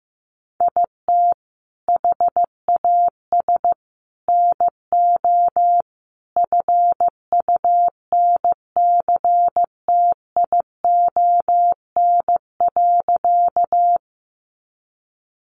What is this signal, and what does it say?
Morse code, heard as a keyed tone, about 15 wpm, 700 Hz. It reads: IT HAS NO FUNCTION.
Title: Sentences from Top 1100 Words 15wpm